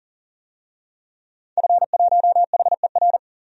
Morse code, heard as a keyed tone, about 40 wpm, 700 Hz. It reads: F1HER
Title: DXCC-100 Call Signs - 5 Characters - 40wpm